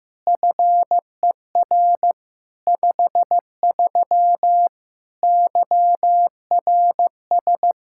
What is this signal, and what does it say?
Morse code, heard as a keyed tone, about 15 wpm, 700 Hz. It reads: FER 53 YRS
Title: Practice QSO - Exchange 3 - More Info - 15wpm